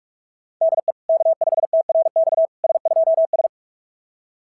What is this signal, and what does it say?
Morse code, heard as a keyed tone, about 46 wpm, 660 Hz. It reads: DE K5TRX S2S
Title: SOTA QSO Practice 45wpm